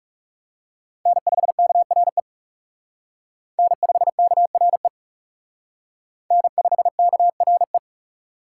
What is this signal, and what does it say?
Morse code, heard as a keyed tone, about 38 wpm, 700 Hz. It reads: N5KRE N5KRE N5KRE